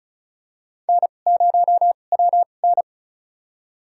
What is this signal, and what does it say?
Morse code, heard as a keyed tone, about 36 wpm, spaced as for 25 wpm, 690 Hz. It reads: N0WN
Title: USA Call Signs - 4 Characters - 2x Character Spacing - 35wpm